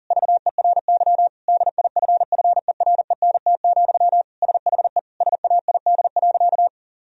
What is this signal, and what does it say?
Morse code, heard as a keyed tone, about 40 wpm, 700 Hz. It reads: VERY DIFFERENT, SHE SAID.
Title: Sentences from Top 1000 Words - Review Part 3 - 40wpm